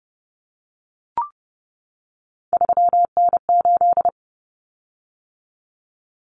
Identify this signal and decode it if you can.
Morse code, heard as a keyed tone, about 30 wpm, 700 Hz. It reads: 3N8
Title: Common Cut Numbers - 3-Digit 30wpm